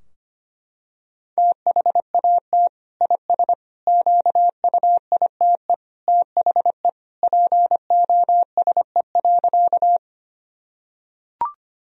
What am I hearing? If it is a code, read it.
Morse code, heard as a keyed tone, about 25 wpm, 700 Hz. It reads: THAT IS QUITE THE POSE.
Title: Sentences from Top 1000 Words - Review Part 2 - 25wpm